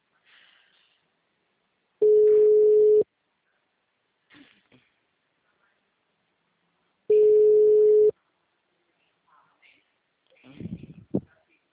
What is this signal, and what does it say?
Ringback tone (Europe, 425 Hz), with speech.